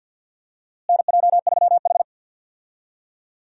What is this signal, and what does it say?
Morse code, heard as a keyed tone, about 51 wpm, 700 Hz. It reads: NJ3H
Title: USA Call Signs - 4 Characters - Repeated 3x - 50wpm